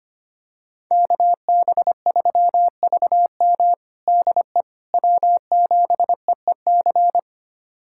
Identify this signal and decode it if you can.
Morse code, heard as a keyed tone, about 25 wpm, 700 Hz. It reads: KB3VM DE W7EEC
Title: USA Callsign DE Callsign - 5 Characters - 25wpm